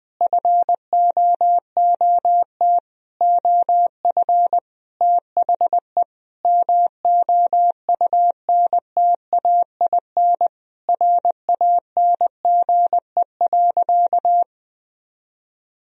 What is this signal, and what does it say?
Morse code, heard as a keyed tone, about 21 wpm, 700 Hz. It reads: FOOT OF THE MOUNTAIN RANGE.
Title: Sentences from Top 1000 Words - Review Part 4 - 20wpm